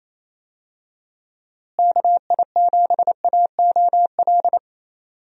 Morse code, heard as a keyed tone, about 27 wpm, 700 Hz. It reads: KI7AOL
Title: USA Call Signs - 6 Characters - Repeated 3x - 28wpm